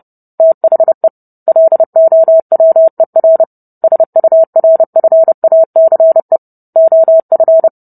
Morse code, heard as a keyed tone, about 30 wpm, 650 Hz. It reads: THE LOWER SURFACE OF